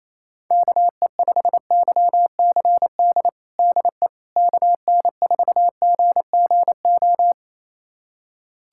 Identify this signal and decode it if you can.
Morse code, heard as a keyed tone, about 27 wpm, 700 Hz. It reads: KE5YCD DE KN4GGO
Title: USA Callsign DE Callsign - 6 Characters - 28wpm